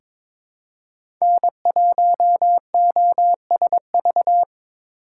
Morse code, heard as a keyed tone, about 22 wpm, 700 Hz. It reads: N1OSV
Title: USA Call Signs - 4x Character Spacing - 22wpm